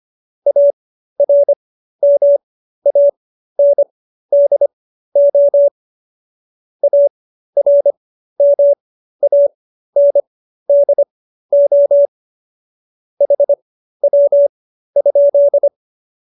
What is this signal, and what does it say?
Morse code, heard as a keyed tone, about 25 wpm, spaced as for 13 wpm, 570 Hz. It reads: ARMANDO ARMANDO HW?
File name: qso215b 13